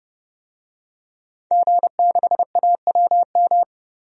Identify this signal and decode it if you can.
Morse code, heard as a keyed tone, about 32 wpm, 700 Hz. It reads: G6AWM
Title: DXCC-100 Call Signs - 5 Characters - 30wpm